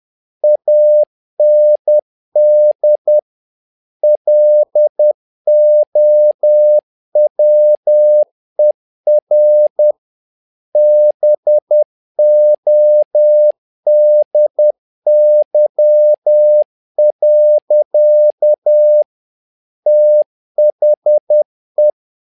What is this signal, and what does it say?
Morse code, heard as a keyed tone, about 10 wpm, 600 Hz. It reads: AND LOWER BODY. THE
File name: learning center / tower trouble / tower trouble 10wpm